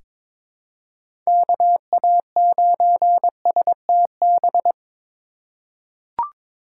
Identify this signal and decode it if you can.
Morse code, heard as a keyed tone, about 22 wpm, 700 Hz. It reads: KA9STB
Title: USA Call Signs - 6 Characters - 4x Character Spacing - 22wpm